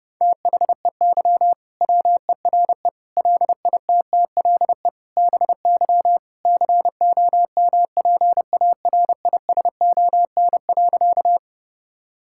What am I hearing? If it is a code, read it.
Morse code, heard as a keyed tone, about 32 wpm, 700 Hz. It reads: THEY WERE LITTLE BY COMPARISON.
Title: Sentences from Top 1600 Words 30wpm